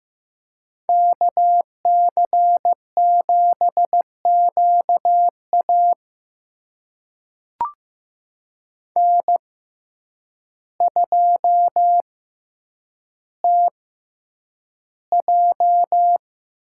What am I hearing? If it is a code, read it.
Morse code, heard as a keyed tone, about 15 wpm, spaced as for 8 wpm, 700 Hz. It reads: KC7QA N2TJ
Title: USA Call Signs - 6x Character Spacing - 15wpm